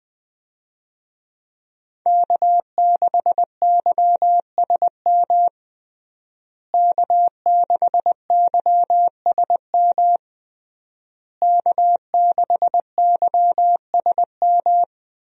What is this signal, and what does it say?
Morse code, heard as a keyed tone, about 21 wpm, 700 Hz. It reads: K6YSM K6YSM K6YSM
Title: USA Call Signs - 5 Characters - Repeated 3x - 20wpm